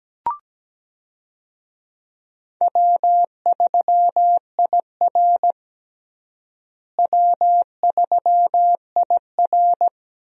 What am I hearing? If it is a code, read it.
Morse code, heard as a keyed tone, about 17 wpm, 700 Hz. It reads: W3IR W3IR
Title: USA Call Signs - 4 Characters - Repeated 3x - 17wpm